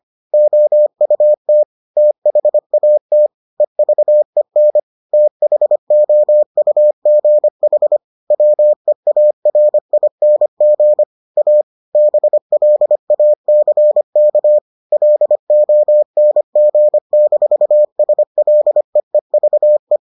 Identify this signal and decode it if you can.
Morse code, heard as a keyed tone, about 25 wpm, 600 Hz. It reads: OUT THAT EVEN THOUGH WEARING A BLACK LONG-SLEEVE